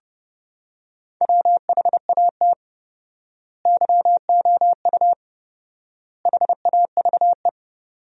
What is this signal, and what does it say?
Morse code, heard as a keyed tone, about 30 wpm, 700 Hz. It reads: WHAT YOU HAVE
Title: 4x Word-Spacing - Sentences from Top 200 Words - 30wpm